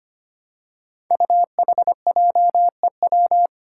Morse code, heard as a keyed tone, about 25 wpm, 700 Hz. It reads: UHJEW
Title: Five Challenging Letter-Number Combinations 25wpm